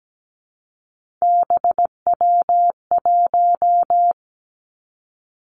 Morse code, heard as a keyed tone, about 17 wpm, 700 Hz. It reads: BW1
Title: Three Challenging Letter-Number Combinations 17wpm